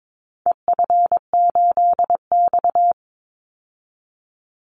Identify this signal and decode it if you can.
Morse code, heard as a keyed tone, about 22 wpm, 700 Hz. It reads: EF8X
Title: Call Signs - 22wpm